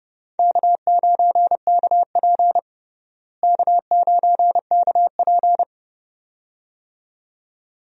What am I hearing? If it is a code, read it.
Morse code, heard as a keyed tone, about 30 wpm, 700 Hz. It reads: K9KP K9KP
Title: USA Call Signs - 4 Characters - Repeated 2x - 30wpm